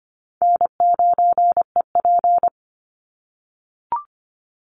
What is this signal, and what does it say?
Morse code, heard as a keyed tone, about 25 wpm, 700 Hz. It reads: N9EP
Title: Easy Call Signs - 25wpm